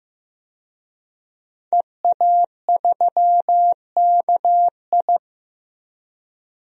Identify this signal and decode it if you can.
Morse code, heard as a keyed tone, about 15 wpm, 700 Hz. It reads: EA3KI
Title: DXCC-100 Call Signs - 5 Characters - 15wpm